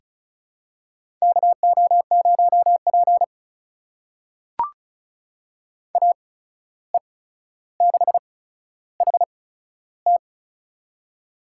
Morse code, heard as a keyed tone, about 36 wpm, spaced as for 14 wpm, 700 Hz. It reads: KO0P AE6HT
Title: USA Call Signs - 8x Character Spacing - 35wpm